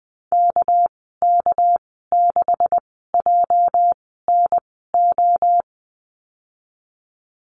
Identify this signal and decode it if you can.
Morse code, heard as a keyed tone, about 20 wpm, spaced as for 14 wpm, 700 Hz. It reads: KK6JNO
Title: 219 Lesson 35g - Number 0 - Call Signs 20wpm